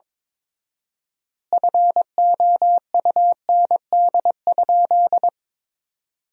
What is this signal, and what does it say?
Morse code, heard as a keyed tone, about 22 wpm, 700 Hz. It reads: FOUND?
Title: 4x Word-Spacing - Sentences from Top 300 Words - 22wpm